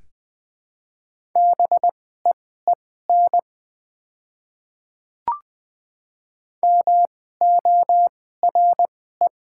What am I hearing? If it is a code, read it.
Morse code, heard as a keyed tone, about 20 wpm, spaced as for 14 wpm, 700 Hz. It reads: BEEN MORE
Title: 2x Character Spacing - Top 300 Words - 20wpm